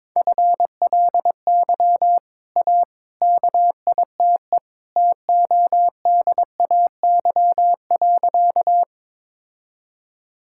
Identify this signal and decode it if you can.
Morse code, heard as a keyed tone, about 22 wpm, 700 Hz. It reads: FLY A KITE TODAY.
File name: practice / 22wpm / Easy Sentences with Hints - 22wpm